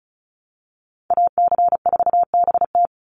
Morse code, heard as a keyed tone, about 34 wpm, 700 Hz. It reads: AC4BT
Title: Call Signs Repeated 2x - 35wpm